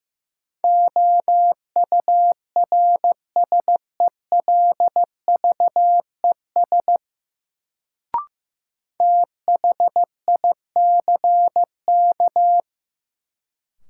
Morse code, heard as a keyed tone, about 15 wpm, 700 Hz. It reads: OURSELVES THICK
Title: Sentences from Top 1600 Words - New Word Drill - 15wpm